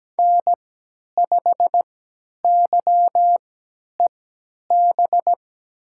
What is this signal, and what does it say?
Morse code, heard as a keyed tone, about 17 wpm, spaced as for 10 wpm, 700 Hz. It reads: N5YEB
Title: USA Call Signs - 5 Characters - 3x Character Spacing - 17wpm